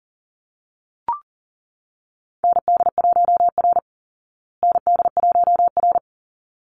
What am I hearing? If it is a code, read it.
Morse code, heard as a keyed tone, about 40 wpm, 700 Hz. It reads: ND1R ND1R
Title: Call Signs Repeated 3x - 40wpm